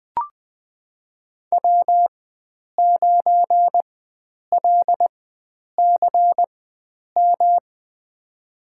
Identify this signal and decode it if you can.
Morse code, heard as a keyed tone, about 20 wpm, spaced as for 9 wpm, 700 Hz. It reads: W9LCM
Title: USA Call Signs - 5 Characters - 4x Character Spacing - 20wpm